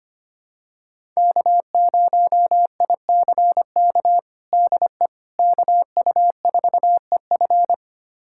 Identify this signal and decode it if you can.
Morse code, heard as a keyed tone, about 25 wpm, 700 Hz. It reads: K0ICK DE KU4EF
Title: USA Callsign DE Callsign - 5 Characters - 25wpm